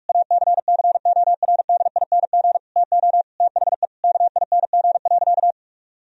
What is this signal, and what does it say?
Morse code, heard as a keyed tone, about 46 wpm, 700 Hz. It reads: ACCORDING TO THE KING.
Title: Sentences from Top 1000 Words - Review Part 4 - 45wpm